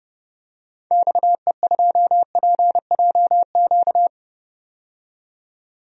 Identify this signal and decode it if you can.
Morse code, heard as a keyed tone, about 31 wpm, 700 Hz. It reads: XE2PJQ
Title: DXCC-100 Call Signs - 6 Characters - 30wpm